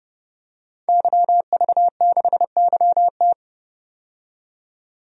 Morse code, heard as a keyed tone, about 29 wpm, 700 Hz. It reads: YV6YT
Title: DXCC-100 Call Signs - 5 Characters - 30wpm